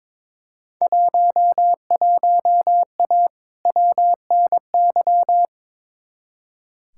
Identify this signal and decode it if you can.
Morse code, heard as a keyed tone, about 21 wpm, 700 Hz. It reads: 11A WNY
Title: Field Day Exchanges Encore - 22wpm